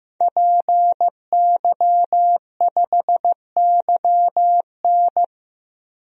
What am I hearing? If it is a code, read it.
Morse code, heard as a keyed tone, about 15 wpm, 700 Hz. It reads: PY5YN
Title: DXCC-100 Call Signs - 5 Characters - 15wpm